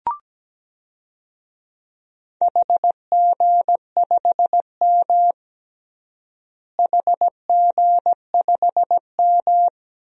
Morse code, heard as a keyed tone, about 17 wpm, 700 Hz. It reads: HG5M HG5M
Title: DXCC-100 Call Signs - 4 Characters - Repeated 2x - 17wpm